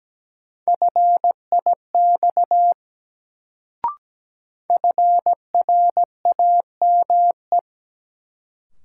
Morse code, heard as a keyed tone, about 17 wpm, 700 Hz. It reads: FIX FRAME
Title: Sentences from Top 1400 Words - New Word Drill Alt - 17wpm